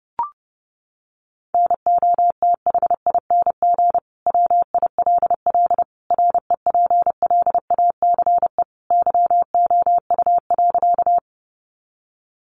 Morse code, heard as a keyed tone, about 30 wpm, 700 Hz. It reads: NOTHING WILL REPLACE YOU.
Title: Sentences from Top 1400 Words 30wpm